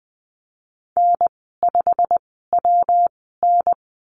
Morse code, heard as a keyed tone, about 20 wpm, spaced as for 15 wpm, 700 Hz. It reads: N5WN
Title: 154 Lesson 25g - Letter p - Call Signs 20wpm